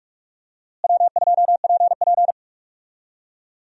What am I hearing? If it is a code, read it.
Morse code, heard as a keyed tone, about 47 wpm, 700 Hz. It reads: W2PP
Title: USA Call Signs - 4 Characters - Repeated 3x - 45wpm